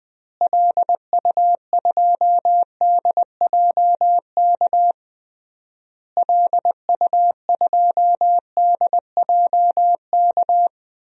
Morse code, heard as a keyed tone, about 20 wpm, 700 Hz. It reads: LU2DJK LU2DJK
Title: DXCC-100 Call Signs - 6 Characters - Repeated 3x - 20wpm